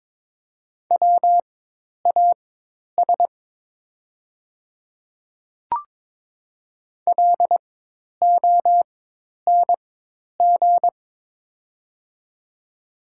Morse code, heard as a keyed tone, about 21 wpm, spaced as for 10 wpm, 700 Hz. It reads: WAS LONG
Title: 4x Character Spacing - Top 200 Words - 22wpm